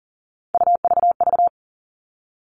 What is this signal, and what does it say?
Morse code, heard as a keyed tone, about 40 wpm, 700 Hz. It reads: UVV
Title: All Cut Numbers - 3-Digit 40wpm